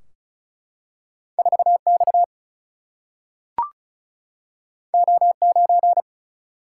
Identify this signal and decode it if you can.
Morse code, heard as a keyed tone, about 36 wpm, 700 Hz. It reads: 4X O9
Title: Single-Letter Single-Number 35wpm